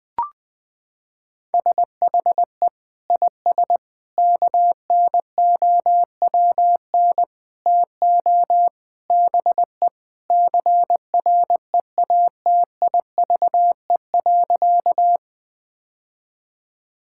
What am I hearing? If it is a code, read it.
Morse code, heard as a keyed tone, about 20 wpm, 700 Hz. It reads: SHE IS KNOWN TO BE CREATIVE.
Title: Sentences from Top 1000 Words - Review Part 5 - 20wpm